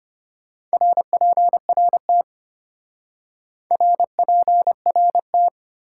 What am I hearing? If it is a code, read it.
Morse code, heard as a keyed tone, about 33 wpm, 700 Hz. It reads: RPRT RPRT